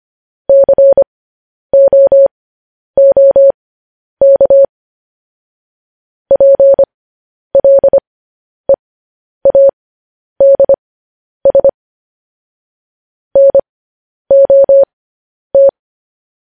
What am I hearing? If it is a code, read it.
Morse code, heard as a keyed tone, about 25 wpm, spaced as for 10 wpm, 550 Hz. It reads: COOK PLEADS NOT